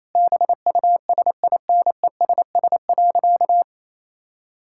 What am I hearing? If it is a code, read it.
Morse code, heard as a keyed tone, about 27 wpm, 700 Hz. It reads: BUSINESS.
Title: Sentences from Top 1000 Words - Review Part 6 - 28wpm